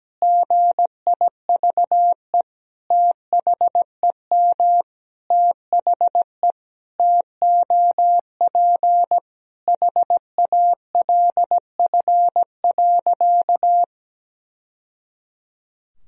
Morse code, heard as a keyed tone, about 17 wpm, 700 Hz. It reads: GIVE THEM THE TOP HALF.
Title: Sentences from Top 1000 Words - Review Part 2 - 17wpm